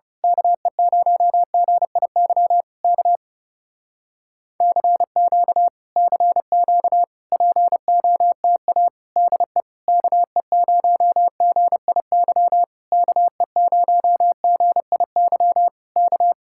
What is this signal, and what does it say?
Morse code, heard as a keyed tone, about 28 wpm, 700 Hz. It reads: KE0GIY K CQ CQ POTA DE KE0GIY KE0GIY K
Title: Speed-Racing Common QSO Phrases Encore - 30wpm